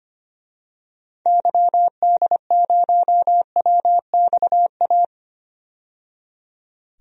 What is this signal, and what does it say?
Morse code, heard as a keyed tone, about 25 wpm, 690 Hz. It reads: YD0WXA
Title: DXCC-100 Call Signs - 6 Characters - 25wpm